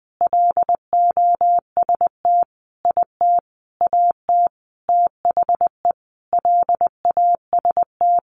Morse code, heard as a keyed tone, about 19 wpm, 700 Hz. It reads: LOST IT AT THE LAST
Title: Sentences from Top 1400 Words 20wpm